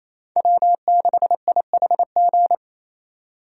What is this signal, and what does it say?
Morse code, heard as a keyed tone, about 28 wpm, 700 Hz. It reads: W6IHG
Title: K1USN SST QSO Practice - 28wpm